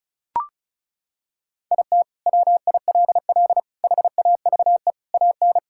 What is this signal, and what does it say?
Morse code, heard as a keyed tone, about 36 wpm, 700 Hz. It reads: IT WILL HAVE AN
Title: Sentences from Top 1000 Words - Review Part 6 - 35wpm